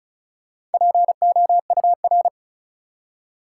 Morse code, heard as a keyed tone, about 36 wpm, 700 Hz. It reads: POUR
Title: Four Letter Words - 35wpm